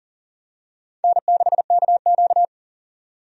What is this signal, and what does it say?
Morse code, heard as a keyed tone, about 40 wpm, 700 Hz. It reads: N6KQ
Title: USA Call Signs - 4 Characters - 40wpm